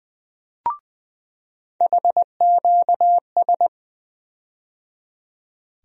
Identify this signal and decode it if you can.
Morse code, heard as a keyed tone, about 20 wpm, 700 Hz. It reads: HQS
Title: USA Call Sign Postfixes - 3 Letters 20wpm